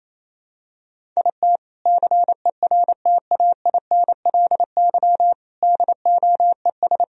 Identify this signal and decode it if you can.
Morse code, heard as a keyed tone, about 28 wpm, 700 Hz. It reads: IT CERTAINLY DOES
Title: Sentences from Top 1000 Words - Review Part 1 - 28wpm